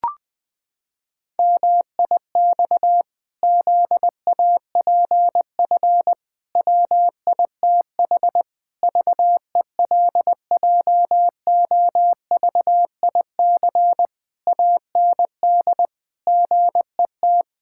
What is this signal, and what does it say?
Morse code, heard as a keyed tone, about 20 wpm, 700 Hz. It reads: MIX ZAPF WITH VELJOVIC AND GET
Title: Pangram Challenge v2 - 20wpm